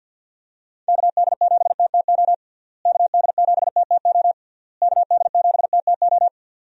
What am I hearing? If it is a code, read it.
Morse code, heard as a keyed tone, about 46 wpm, 700 Hz. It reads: KD7TTO KD7TTO KD7TTO
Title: USA Call Signs - 6 Characters - Repeated 3x - 50wpm